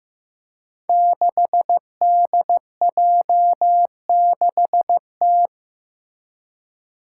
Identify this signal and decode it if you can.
Morse code, heard as a keyed tone, about 15 wpm, 700 Hz. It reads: 6DJ6T
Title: Five Challenging Letter-Number Combinations 15wpm